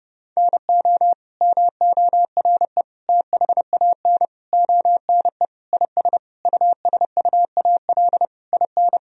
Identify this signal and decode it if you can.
Morse code, heard as a keyed tone, about 32 wpm, 700 Hz. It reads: NO MORE THAN ONE IS USUAL IN
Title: Sentences from Top 1000 Words - Review Part 2 - 30wpm